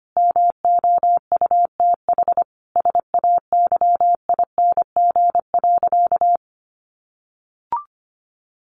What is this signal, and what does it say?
Morse code, heard as a keyed tone, about 25 wpm, 700 Hz. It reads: MOUTH SAYING.
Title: A Sound Of Thunder 25wpm